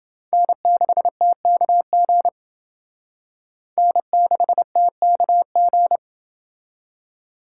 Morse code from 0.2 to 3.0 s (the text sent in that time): N6TKG